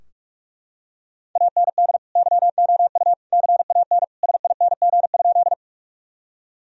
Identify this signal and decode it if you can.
Morse code, heard as a keyed tone, about 44 wpm, 700 Hz. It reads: AND YOU CAN SING?
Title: Sentences from Top 1000 Words - Review Part 2 - 45wpm